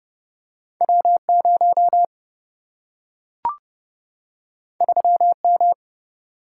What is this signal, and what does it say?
Morse code, heard as a keyed tone, about 30 wpm, 700 Hz. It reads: W 0 3 M
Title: Single-Letter Single-Number 30wpm